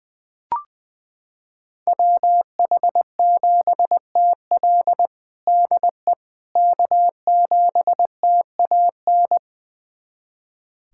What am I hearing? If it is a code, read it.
Morse code, heard as a keyed tone, about 20 wpm, 700 Hz. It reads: WH7TL DE K7TAN
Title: USA Callsign DE Callsign - 5 Characters - 20wpm